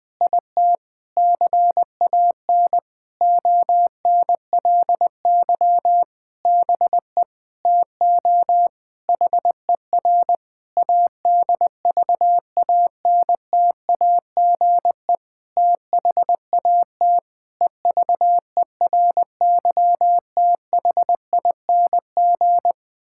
Morse code, heard as a keyed tone, about 20 wpm, 700 Hz. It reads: IT CAN ONLY BE TO HER ADVANTAGE THAT EVERYTHING
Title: Sentences from Top 1000 Words - Review Part 6 - 20wpm